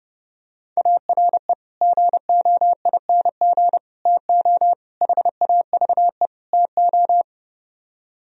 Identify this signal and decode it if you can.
Morse code, heard as a keyed tone, about 31 wpm, 700 Hz. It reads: ARE GOING TO HAVE TO